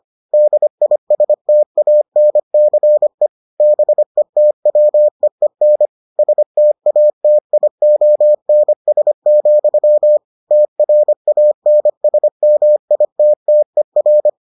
Morse code, heard as a keyed tone, about 25 wpm, 600 Hz. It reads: DISTANCE BETWEEN STATIONS, TRANSMITTER